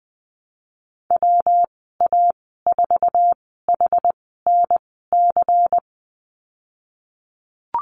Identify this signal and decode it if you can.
Morse code, heard as a keyed tone, about 20 wpm, spaced as for 14 wpm, 700 Hz. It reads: WA4HNC
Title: 231 Lesson 37g - Letter x - Call Signs 20wpm